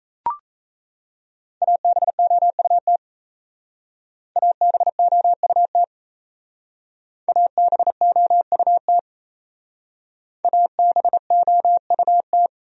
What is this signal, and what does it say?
Morse code, heard as a keyed tone, about 37 wpm, 700 Hz. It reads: ABOUT ABOUT ABOUT ABOUT